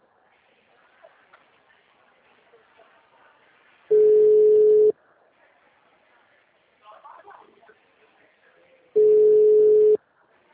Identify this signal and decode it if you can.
Ringback tone (Europe, 425 Hz), with speech.